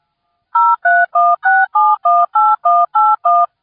DTMF, heard as telephone touch tones, tones of about 220 ms. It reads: 0316718181